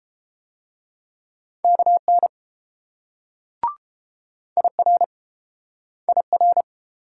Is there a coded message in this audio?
Yes: Morse code from 1.6 to 6.6 s, heard as a keyed tone, about 30 wpm, 700 Hz. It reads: KN IR IR